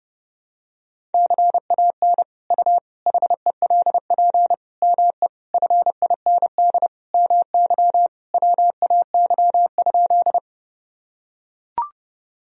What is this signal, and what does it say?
Morse code, heard as a keyed tone, about 32 wpm, 700 Hz. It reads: CAN U HELP ME FIND MY WAY?
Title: Sentences from Top 200 Words and Abbreviations 30wpm